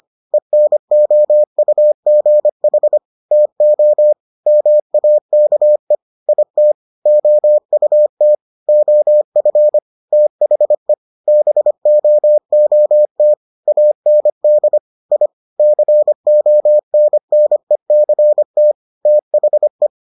Morse code, heard as a keyed tone, about 25 wpm, 600 Hz. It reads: ENOUGH TO MAKE IT OUT OF THE BOOT AND I CONNECT THE